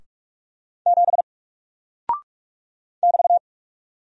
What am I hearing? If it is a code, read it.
Morse code, heard as a keyed tone, about 44 wpm, 700 Hz. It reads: 7 <BT>
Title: Single Character (Letter:Number:Punctuation:Prosign) 45wpm